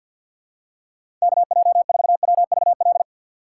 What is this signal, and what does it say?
Morse code, heard as a keyed tone, about 53 wpm, 700 Hz. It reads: KJ4WVL
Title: USA Call Signs - 6 Characters - 8x Character Spacing - 50wpm